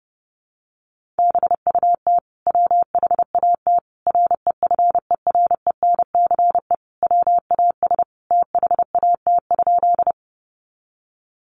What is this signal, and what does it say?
Morse code, heard as a keyed tone, about 30 wpm, 700 Hz. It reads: BUT WHAT REFERENCE WAS THAT?
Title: Sentences from Top 1200 Words 30wpm